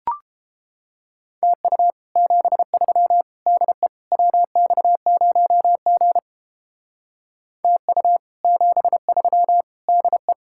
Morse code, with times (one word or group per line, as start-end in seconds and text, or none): 1.4-1.9 TU
2.2-3.2 73
3.5-3.9 DE
4.1-6.2 WX0G
7.6-8.2 TU
8.4-9.6 73
9.9-10.3 DE